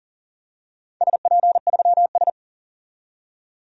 Morse code, heard as a keyed tone, about 40 wpm, 700 Hz. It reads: SP3S
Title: DXCC-100 Call Signs - 4 Characters - 40wpm